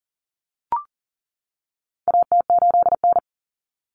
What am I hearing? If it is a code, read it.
Morse code, heard as a keyed tone, about 40 wpm, 700 Hz. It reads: AT8N
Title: Common Cut Numbers - 4-Digit 40wpm